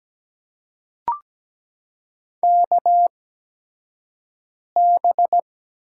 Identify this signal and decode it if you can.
Morse code, heard as a keyed tone, about 17 wpm, spaced as for 5 wpm, 700 Hz. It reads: KB
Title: USA Call Signs - 6 Characters - 8x Character Spacing - 17wpm